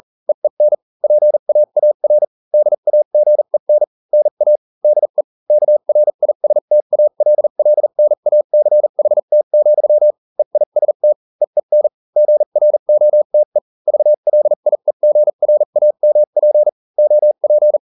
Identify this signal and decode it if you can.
Morse code, heard as a keyed tone, about 40 wpm, spaced as for 35 wpm, 600 Hz. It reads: EEN PAAR DAGEN NA DE KRISTALLNACHT, EIST EEN GROTE VLIEGRAMP OP